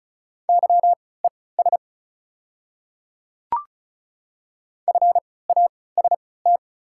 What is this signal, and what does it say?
Morse code, heard as a keyed tone, about 36 wpm, spaced as for 20 wpm, 700 Hz. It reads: YES FAST
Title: 3x Character Spacing - Top 500 Words - 35wpm